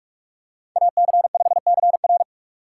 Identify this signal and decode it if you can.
Morse code, heard as a keyed tone, about 46 wpm, 700 Hz. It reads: AC5CR
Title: USA Call Signs - 5 Characters - 4x Character Spacing - 45wpm